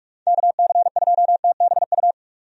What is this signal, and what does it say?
Morse code, heard as a keyed tone, about 44 wpm, 700 Hz. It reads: KC2TBU
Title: USA Call Signs - 6 Characters - 45wpm